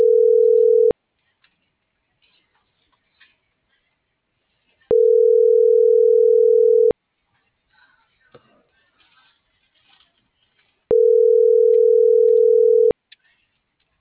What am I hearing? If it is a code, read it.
Ringback tone (North America), 440 + 480 Hz.